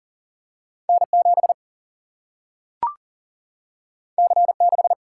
Morse code, heard as a keyed tone, about 40 wpm, 700 Hz. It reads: N 7 C 6
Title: Single-Letter Single-Number 40wpm